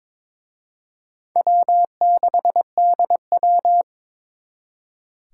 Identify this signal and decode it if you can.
Morse code, heard as a keyed tone, about 26 wpm, 700 Hz. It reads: W6DW
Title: Speed Racing - USA 1x2 Call Signs 22wpm